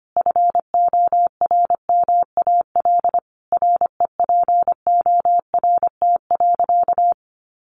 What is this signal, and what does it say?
Morse code, heard as a keyed tone, about 25 wpm, 700 Hz. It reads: FORMAL REPORT.
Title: Sentences from Top 1300 Words 25wpm